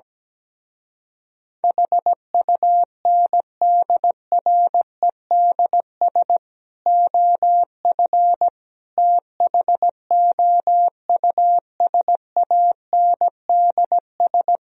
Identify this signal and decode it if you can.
Morse code, heard as a keyed tone, about 17 wpm, 700 Hz. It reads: HUNDREDS OF THOUSANDS
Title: Speed-Racing - Sets of 4 Words 17wpm